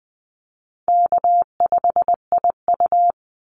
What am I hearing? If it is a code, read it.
Morse code, heard as a keyed tone, about 20 wpm, 700 Hz. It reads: K5IU
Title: Call Signs Repeated 2x - 20wpm